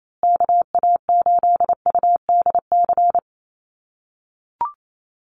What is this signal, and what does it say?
Morse code, heard as a keyed tone, about 28 wpm, 700 Hz. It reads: KA8UDC
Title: USA Call Signs Repeated 2x 28wpm